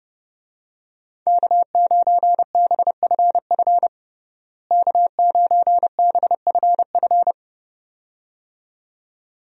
Morse code, heard as a keyed tone, about 32 wpm, 700 Hz. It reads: K9BFF K9BFF
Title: USA Call Signs - 5 Characters - Repeated 2x - 30wpm